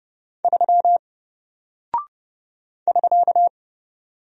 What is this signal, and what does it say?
Morse code, heard as a keyed tone, about 30 wpm, 700 Hz. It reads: 3 <SK>
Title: Single Character (Letter:Number:Punctuation:Prosign) 30wpm